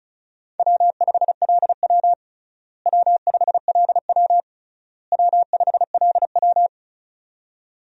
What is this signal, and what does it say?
Morse code, heard as a keyed tone, about 36 wpm, 700 Hz. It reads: W5LW W5LW W5LW
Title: USA Call Signs - 4 Characters - Repeated 3x - 35wpm